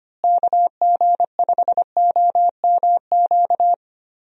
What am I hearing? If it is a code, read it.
Morse code, heard as a keyed tone, about 25 wpm, 700 Hz. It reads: KG5OMQ
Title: USA Call Signs - 6 Characters - Repeated 2x - 25wpm